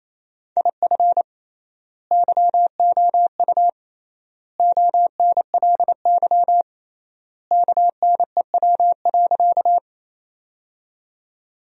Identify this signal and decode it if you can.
Morse code, heard as a keyed tone, about 28 wpm, 700 Hz. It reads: IF YOU ONLY KNEW.
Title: 3x Word-Spacing - Sentences from Top 400 Words - 28wpm